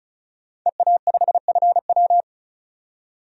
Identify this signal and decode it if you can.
Morse code, heard as a keyed tone, about 36 wpm, 700 Hz. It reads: EA5FW
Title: DXCC-100 Call Signs - 5 Characters - 35wpm